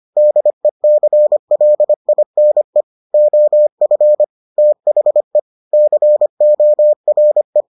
Morse code, heard as a keyed tone, about 25 wpm, 600 Hz. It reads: DECLINE OF THE CORE